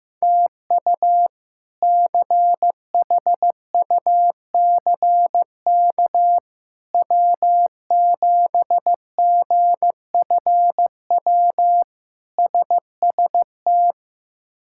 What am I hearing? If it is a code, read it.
Morse code, heard as a keyed tone, about 15 wpm, 700 Hz. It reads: TU CHUCK W7GFW SST
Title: K1USN SST QSO Practice - 15wpm